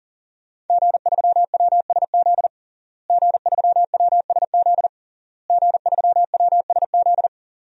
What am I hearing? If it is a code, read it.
Morse code, heard as a keyed tone, about 40 wpm, 700 Hz. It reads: G3WSZ G3WSZ G3WSZ
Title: DXCC-100 Call Signs - 5 Characters - Repeated 3x - 40wpm